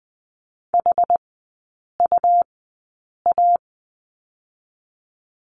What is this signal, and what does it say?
Morse code, heard as a keyed tone, about 20 wpm, spaced as for 8 wpm, 690 Hz. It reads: HUA
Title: 133 Lesson 22e - Question Mark - Three Characters 20wpm